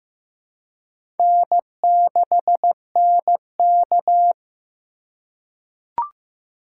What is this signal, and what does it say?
Morse code, heard as a keyed tone, about 15 wpm, 700 Hz. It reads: N6NK
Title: USA Call Signs - 3x Character Spacing - 15wpm